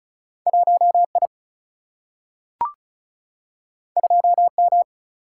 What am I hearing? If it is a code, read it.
Morse code, heard as a keyed tone, about 36 wpm, 700 Hz. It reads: 1I 2M
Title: Single-Letter Single-Number 35wpm